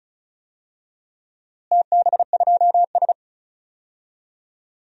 Morse code, heard as a keyed tone, about 34 wpm, 700 Hz. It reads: TB2S
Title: DXCC-100 Call Signs - 4 Characters - Repeated 3x - 35wpm